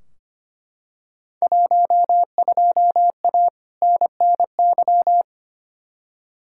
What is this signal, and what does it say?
Morse code, heard as a keyed tone, about 25 wpm, 700 Hz. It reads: 12A NNY
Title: Field Day Exchanges Encore - 25wpm